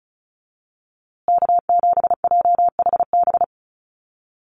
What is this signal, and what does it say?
Morse code, heard as a keyed tone, about 34 wpm, 700 Hz. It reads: K7JHB
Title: USA Call Signs Repeated 2x 35wpm